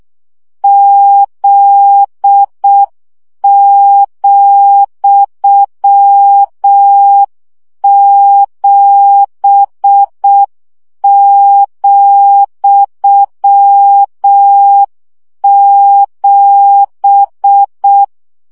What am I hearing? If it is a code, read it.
Morse code, heard as a keyed tone, about 6 wpm, 800 Hz. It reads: Z,7,7
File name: CW gyakorlas z 7 vesszo 5 baud